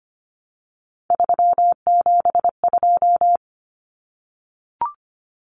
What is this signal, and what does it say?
Morse code, heard as a keyed tone, about 25 wpm, 700 Hz. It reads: 372
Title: Common Cut Numbers - 3-Digit 25wpm